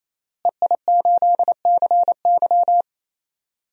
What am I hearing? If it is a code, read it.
Morse code, heard as a keyed tone, about 28 wpm, 700 Hz. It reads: EI8CY
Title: DXCC-100 Call Signs - 5 Characters - Repeated 3x - 28wpm